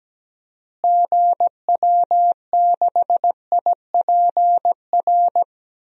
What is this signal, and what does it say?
Morse code, heard as a keyed tone, about 17 wpm, 700 Hz. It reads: GW6IPR
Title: DXCC-100 Call Signs - 6 Characters - Repeated 3x - 17wpm